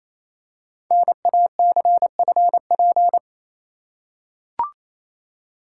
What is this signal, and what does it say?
Morse code, heard as a keyed tone, about 27 wpm, 700 Hz. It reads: NACFP
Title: Code Groups - 5 Letter:Numbers - 28wpm